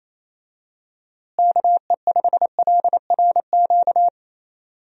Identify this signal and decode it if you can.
Morse code, heard as a keyed tone, about 28 wpm, 700 Hz. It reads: KE5LRQ
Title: USA Call Signs - 6 Characters - Repeated 2x - 28wpm